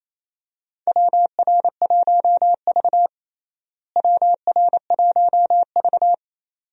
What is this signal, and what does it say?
Morse code, heard as a keyed tone, about 28 wpm, 700 Hz. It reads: WR1V WR1V
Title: USA Call Signs - 4 Characters - Repeated 2x - 28wpm